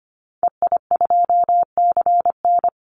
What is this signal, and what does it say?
Morse code, heard as a keyed tone, about 25 wpm, 700 Hz. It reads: EI2CN
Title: Call Signs - 25wpm